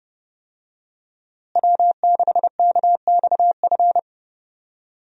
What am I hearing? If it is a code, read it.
Morse code, heard as a keyed tone, about 32 wpm, 700 Hz. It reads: W6KXF
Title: USA Call Signs - 5 Characters - Repeated 2x - 30wpm